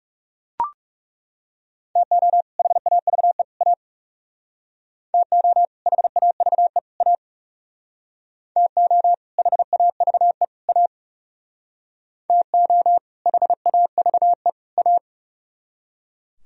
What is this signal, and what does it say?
Morse code, heard as a keyed tone, about 37 wpm, 700 Hz. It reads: TO HAVE A TO HAVE A TO HAVE A TO HAVE A
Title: Speed-Racing - Sets of 3 Words 30wpm